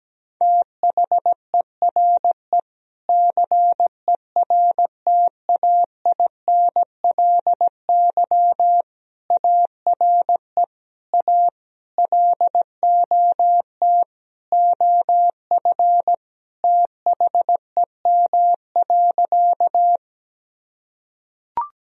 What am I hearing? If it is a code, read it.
Morse code, heard as a keyed tone, about 17 wpm, 700 Hz. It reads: THERE CERTAINLY ARE A LOT OF THEM.
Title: Sentences from Top 1000 Words - Review Part 3 - 17wpm